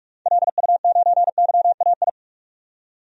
Morse code, heard as a keyed tone, about 43 wpm, 700 Hz. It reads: LU9YAI
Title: DXCC-100 Call Signs - 6 Characters - 45wpm